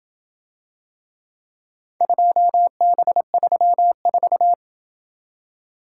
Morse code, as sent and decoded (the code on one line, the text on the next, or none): ..--- -... ...-- ....-
2B34